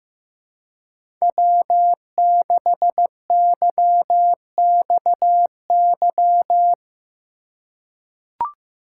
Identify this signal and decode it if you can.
Morse code, heard as a keyed tone, about 15 wpm, 700 Hz. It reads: W6YXY
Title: USA Call Signs - 5 Characters - 4x Character Spacing - 15wpm